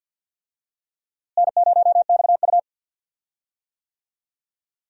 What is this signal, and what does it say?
Morse code, heard as a keyed tone, about 53 wpm, 700 Hz. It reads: N0XU
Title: USA Call Signs - 4 Characters - Repeated 2x - 50wpm